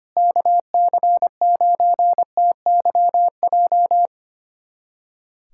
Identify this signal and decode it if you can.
Morse code, heard as a keyed tone, about 25 wpm, 700 Hz. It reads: KC9TYJ
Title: USA Callsign DE Callsign - 6 Characters - 25wpm